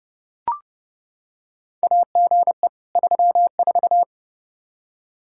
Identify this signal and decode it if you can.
Morse code, heard as a keyed tone, about 30 wpm, 700 Hz. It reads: AGE 34
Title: Practice QSO - Exchange 3 - More Info - 30wpm